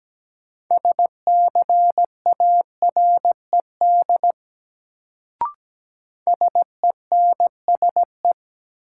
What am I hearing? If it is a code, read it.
Morse code, heard as a keyed tone, about 17 wpm, 700 Hz. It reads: SCARED SENSE
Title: Sentences from Top 700 Words - New Word Drill - 17wpm